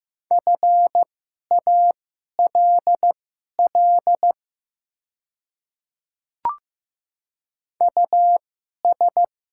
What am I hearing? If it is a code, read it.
Morse code, heard as a keyed tone, about 15 wpm, spaced as for 11 wpm, 700 Hz. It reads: FALL US
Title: 2x Character Spacing - Top 500 Words - 15wpm